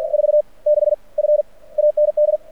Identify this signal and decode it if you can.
Morse code, heard as a keyed tone, about 48 wpm, 600 Hz. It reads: 4 6 F AND
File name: archived / CRSnet